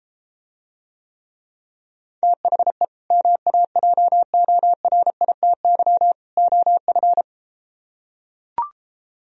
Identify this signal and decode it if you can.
Morse code, heard as a keyed tone, about 33 wpm, 700 Hz. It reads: THE MAJORITY OF